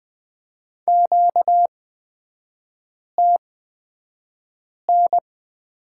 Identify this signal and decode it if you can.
Morse code, heard as a keyed tone, about 20 wpm, spaced as for 5 wpm, 700 Hz. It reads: QTN
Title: 236 Lesson 38e - Letter q - Three Characters 20wpm